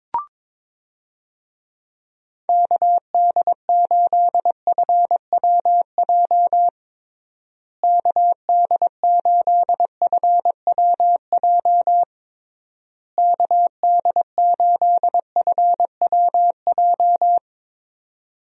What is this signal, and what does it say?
Morse code, heard as a keyed tone, about 22 wpm, 700 Hz. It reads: KD8FWJ KD8FWJ KD8FWJ
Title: USA Call Signs - 6 Characters - Repeated 3x - 22wpm